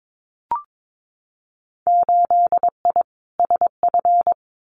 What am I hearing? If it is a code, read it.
Morse code, heard as a keyed tone, about 22 wpm, 700 Hz. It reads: 8I SF
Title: Winter Field Day Exchanges Encore 22wpm